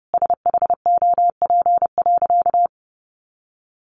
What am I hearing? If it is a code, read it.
Morse code, heard as a keyed tone, about 30 wpm, 700 Hz. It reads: SHOP.
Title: Easy Phrases 30wpm